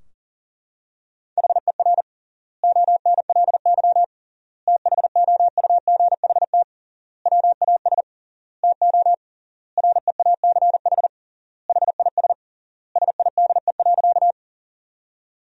Morse code, heard as a keyed tone, about 40 wpm, 700 Hz. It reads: HER ONLY THOUGHT WAS TO REACH HIS SIDE.
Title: 3x Word-Spacing - Sentences from Top 400 Words - 40wpm